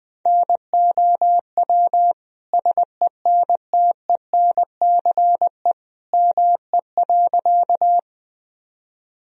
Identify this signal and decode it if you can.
Morse code, heard as a keyed tone, about 20 wpm, 700 Hz. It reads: NOW SENTENCE ME.
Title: Sentences from Top 1000 Words - Review Part 1 - 20wpm